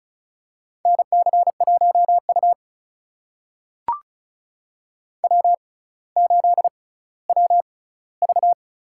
Morse code, heard as a keyed tone, about 36 wpm, spaced as for 18 wpm, 700 Hz. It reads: NC1U W8WV
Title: USA Call Signs - 4 Characters - 6x Character Spacing - 35wpm